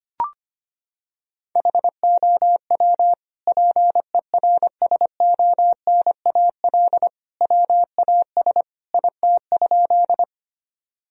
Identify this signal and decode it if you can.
Morse code, heard as a keyed tone, about 25 wpm, 700 Hz. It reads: HOW PERSONAL WAS IT?
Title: Sentences from Top 1000 Words - Review Part 4 - 25wpm